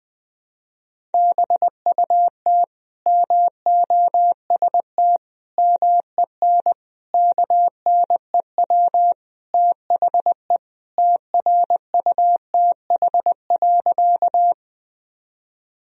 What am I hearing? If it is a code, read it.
Morse code, heard as a keyed tone, about 20 wpm, 700 Hz. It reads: BUT MOST MEN KNEW THE TRUTH.
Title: Sentences from Top 1000 Words - Review Part 1 - 20wpm